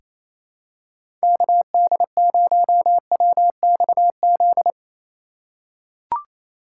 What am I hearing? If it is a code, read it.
Morse code, heard as a keyed tone, about 28 wpm, 700 Hz. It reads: KD0WXZ
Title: USA Call Signs - 6 Characters - 2x Character Spacing - 28wpm